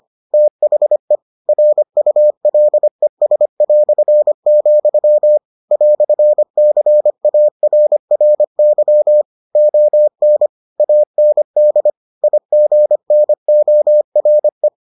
Morse code, heard as a keyed tone, about 25 wpm, 600 Hz. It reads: THE RULES", "CARRY ON AND IGNORE